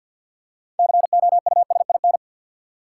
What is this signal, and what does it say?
Morse code, heard as a keyed tone, about 50 wpm, 700 Hz. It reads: COUSIN
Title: Six Letter Words - 50wpm